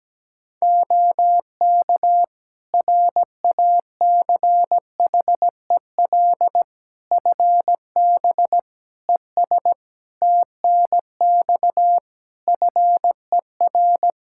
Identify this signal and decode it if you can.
Morse code, heard as a keyed tone, about 17 wpm, 700 Hz. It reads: OK RACHEL FB ES TNX FER
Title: K1USN SST QSO Practice 17wpm